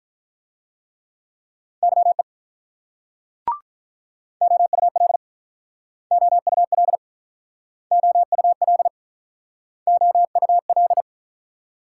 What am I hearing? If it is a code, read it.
Morse code, heard as a keyed tone, about 45 wpm, 700 Hz. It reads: YE OUL OUL OUL OUL